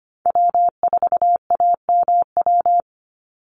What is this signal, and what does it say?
Morse code, heard as a keyed tone, about 25 wpm, 700 Hz. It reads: W4AMW
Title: NA SOTA Chaser Callsigns 25wpm